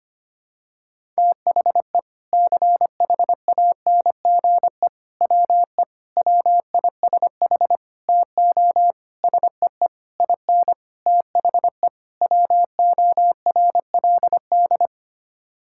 Morse code, heard as a keyed tone, about 25 wpm, 700 Hz. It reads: THE CHANGE WE WISH TO SEE IN THE WORLD
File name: Sets of 9 Words - 25wpm